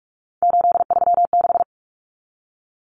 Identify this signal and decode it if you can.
Morse code, heard as a keyed tone, about 44 wpm, 700 Hz. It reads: 836
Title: Numbers - 3-Digit 45wpm